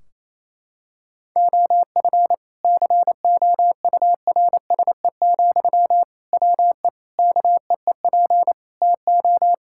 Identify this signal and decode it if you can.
Morse code, heard as a keyed tone, about 28 wpm, 700 Hz. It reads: OF COURSE, WE KEEP TO
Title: Sentences from Top 1600 Words 28wpm